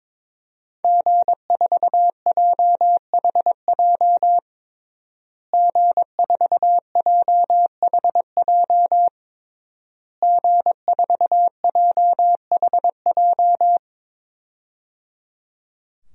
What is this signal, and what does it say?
Morse code, heard as a keyed tone, about 22 wpm, 700 Hz. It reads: G4JHJ G4JHJ G4JHJ
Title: DXCC-100 Call Signs - 5 Characters - Repeated 3x - 22wpm